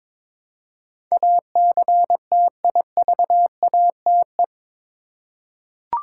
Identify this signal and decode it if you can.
Morse code, heard as a keyed tone, about 22 wpm, 700 Hz. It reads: ACTIVATE
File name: practice / 22wpm / Eight Letter Words - 22wpm